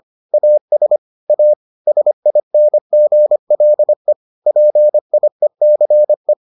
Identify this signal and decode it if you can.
Morse code, heard as a keyed tone, about 25 wpm, 600 Hz. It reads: AS A SINGLE PIECE